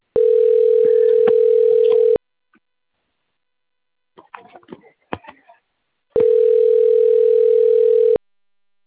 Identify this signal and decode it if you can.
Ringback tone (North America), 440 + 480 Hz.